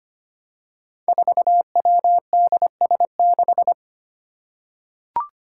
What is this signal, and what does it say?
Morse code, heard as a keyed tone, about 25 wpm, 700 Hz. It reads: 4WDS6
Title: Five Challenging Letter-Number Combinations 25wpm